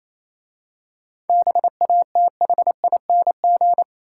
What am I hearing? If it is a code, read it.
Morse code, heard as a keyed tone, about 27 wpm, 690 Hz. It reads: BATHING
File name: Easy Words - Speed-Racing with Hints - 22wpm